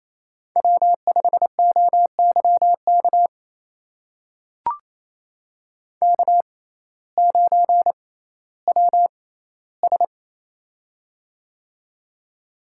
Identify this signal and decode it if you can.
Morse code, heard as a keyed tone, about 28 wpm, spaced as for 15 wpm, 700 Hz. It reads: W5OYK K9WS
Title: USA Call Signs - 6x Character Spacing - 28wpm